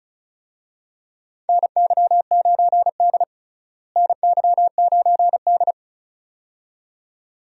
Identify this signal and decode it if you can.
Morse code, heard as a keyed tone, about 36 wpm, 700 Hz. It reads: NY9D NY9D
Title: USA Call Signs - 4 Characters - Repeated 2x - 35wpm